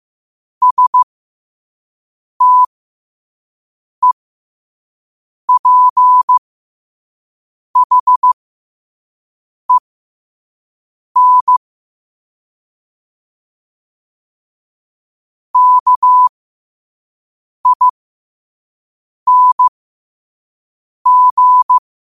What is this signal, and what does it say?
Morse code, heard as a keyed tone, about 15 wpm, spaced as for 5 wpm, 1000 Hz. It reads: STEPHEN KING
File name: qotdfiles / 5wpm